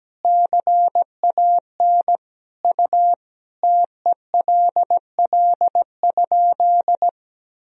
Morse code, heard as a keyed tone, about 17 wpm, 700 Hz. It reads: CAN U TELL?
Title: Sentences from Top 200 Words and Abbreviations 17wpm